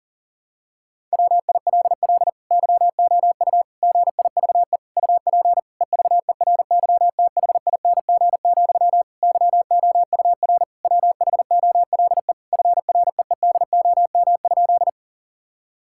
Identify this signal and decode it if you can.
Morse code, heard as a keyed tone, about 40 wpm, 700 Hz. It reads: WILL YOU GIVE UP EVERYTHING, YOUR WHOLE FREEDOM?
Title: Sentences from Top 1000 Words - Review Part 5 - 40wpm